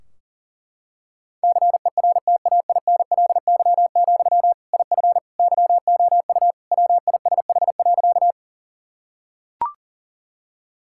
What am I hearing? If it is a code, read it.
Morse code, heard as a keyed tone, about 40 wpm, 700 Hz. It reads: CERTAINLY, IF YOU WISH.
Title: Sentences from Top 1600 Words 40wpm